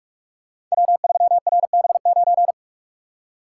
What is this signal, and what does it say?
Morse code, heard as a keyed tone, about 44 wpm, 700 Hz. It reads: W3RB9
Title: Code Groups - 5 Letter:Numbers - 45wpm